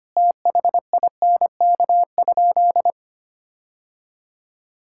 Morse code, heard as a keyed tone, about 25 wpm, 700 Hz. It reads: THINK?
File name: practice / 25wpm / Sentences with 3-Word Phrases - 25wpm